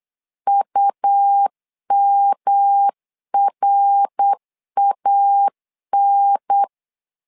Morse code, heard as a keyed tone, about 8 wpm, 800 Hz. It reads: UMRAN
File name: LEZIONI / 4 Q1 8wpm 3035